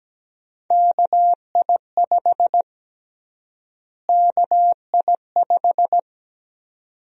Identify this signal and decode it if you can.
Morse code, heard as a keyed tone, about 17 wpm, 700 Hz. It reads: KI5 KI5
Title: USA Call Sign Prefixes - Repeated 2x - 17wpm